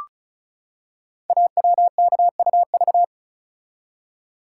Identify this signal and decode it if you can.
Morse code, heard as a keyed tone, about 36 wpm, 700 Hz. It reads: AWKUV
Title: Code Groups - 5 Letters - 35wpm